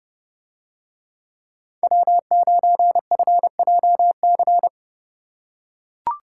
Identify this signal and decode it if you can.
Morse code, heard as a keyed tone, about 27 wpm, 700 Hz. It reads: W9FJC